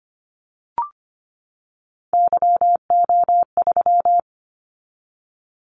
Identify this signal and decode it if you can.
Morse code, heard as a keyed tone, about 25 wpm, 700 Hz. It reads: YO3
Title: Callsign Prefixes for Top 25 DXCC - 25wpm